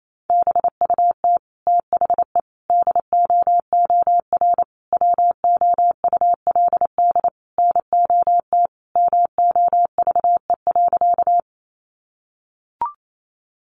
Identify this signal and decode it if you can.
Morse code, heard as a keyed tone, about 28 wpm, 700 Hz. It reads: BUT THE DOOR WOULD NOT MOVE.
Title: Sentences from Top 600 Words 28wpm